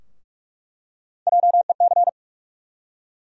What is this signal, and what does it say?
Morse code, heard as a keyed tone, about 46 wpm, 700 Hz. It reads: JEC
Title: USA Call Sign Postfixes - 3 Letters 45wpm